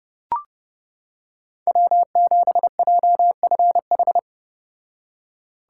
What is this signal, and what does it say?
Morse code, heard as a keyed tone, about 30 wpm, 700 Hz. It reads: W7JFH